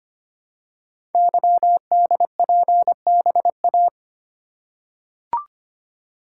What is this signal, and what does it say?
Morse code, heard as a keyed tone, about 25 wpm, 700 Hz. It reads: YDPBA
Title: Code Groups - 5 Letters - 25wpm